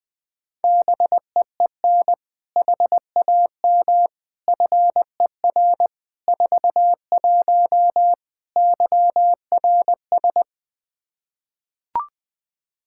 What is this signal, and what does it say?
Morse code, heard as a keyed tone, about 20 wpm, 700 Hz. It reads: BEEN HAM FER 41 YRS
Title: Practice QSO - Exchange 3 - More Info - 20wpm